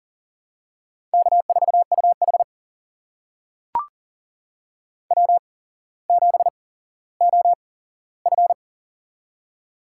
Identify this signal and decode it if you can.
Morse code, heard as a keyed tone, about 40 wpm, spaced as for 17 wpm, 700 Hz. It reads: K4UH W7OF
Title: USA Call Signs - 4 Characters - 8x Character Spacing - 40wpm